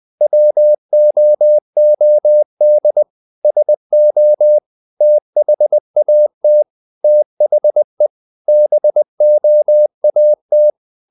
Morse code, heard as a keyed tone, about 20 wpm, 600 Hz. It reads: WOOD SO THAT THE BOAT
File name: the old man and the sea part 2 20wpm